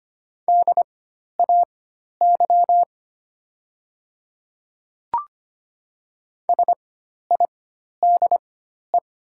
Morse code, heard as a keyed tone, about 25 wpm, spaced as for 12 wpm, 700 Hz. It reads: DAY SIDE
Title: 4x Character Spacing - Top 500 Words - 25wpm